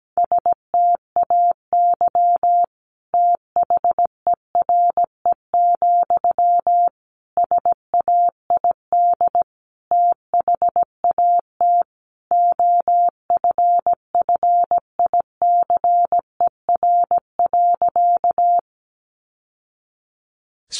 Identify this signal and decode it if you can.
Morse code, heard as a keyed tone, about 17 wpm, 700 Hz. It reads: STAY THERE, SAID THAT OFFICER.
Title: Sentences from Top 1200 Words 17wpm